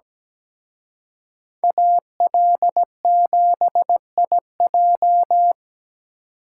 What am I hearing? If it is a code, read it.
Morse code, heard as a keyed tone, about 20 wpm, 700 Hz. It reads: AL7IJ
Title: Speed Racing - USA Call Signs 17wpm